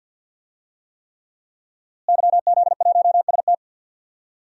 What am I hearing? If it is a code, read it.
Morse code, heard as a keyed tone, about 52 wpm, 700 Hz. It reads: YZ1ST
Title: DXCC-100 Call Signs - 5 Characters - Repeated 3x - 50wpm